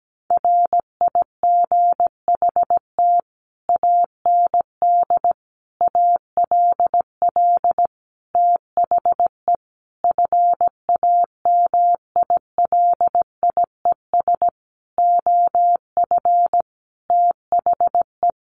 Morse code, heard as a keyed tone, about 17 wpm, 700 Hz. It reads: RIGHT AND ALL THE FAMILIES OF THE
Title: A Sound Of Thunder 17wpm